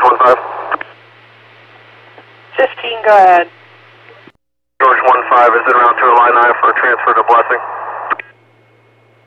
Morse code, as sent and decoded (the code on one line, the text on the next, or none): none